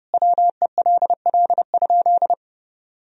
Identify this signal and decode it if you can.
Morse code, heard as a keyed tone, about 30 wpm, 700 Hz. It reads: WELL?
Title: Sentences from Top 1000 Words - Review Part 2 - 30wpm